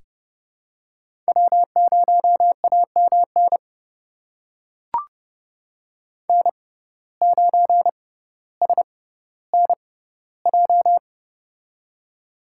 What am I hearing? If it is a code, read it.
Morse code, heard as a keyed tone, about 32 wpm, spaced as for 16 wpm, 700 Hz. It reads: W0AMN N9SNJ
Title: USA Call Signs - 5 Characters - 6x Character Spacing - 30wpm